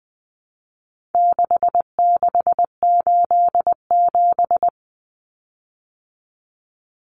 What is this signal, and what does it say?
Morse code, heard as a keyed tone, about 20 wpm, 700 Hz. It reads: 6687
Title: Common Cut Numbers - 4-Digit 20wpm